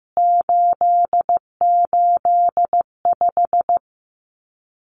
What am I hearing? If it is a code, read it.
Morse code, heard as a keyed tone, about 15 wpm, 700 Hz. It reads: 885
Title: CWT Exchange 15wpm